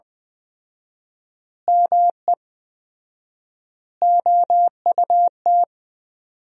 Morse code, heard as a keyed tone, about 20 wpm, 700 Hz. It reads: ME OUT
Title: 4x Word-Spacing - Sentences from Top 400 Words - 20wpm